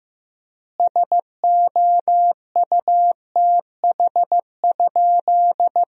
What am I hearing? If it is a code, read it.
Morse code, heard as a keyed tone, about 15 wpm, 700 Hz. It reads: SOUTH?
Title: 3x Word-Spacing - Sentences from Top 400 Words - 15wpm